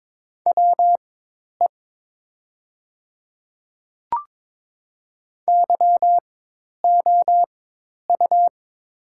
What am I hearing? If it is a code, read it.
Morse code, heard as a keyed tone, about 22 wpm, spaced as for 10 wpm, 700 Hz. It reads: WE YOU
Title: 4x Character Spacing - Top 100 Words - 22wpm